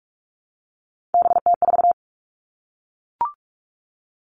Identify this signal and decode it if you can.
Morse code, heard as a keyed tone, about 44 wpm, 700 Hz. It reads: BT4
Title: All Cut Numbers - 3-Digit 45wpm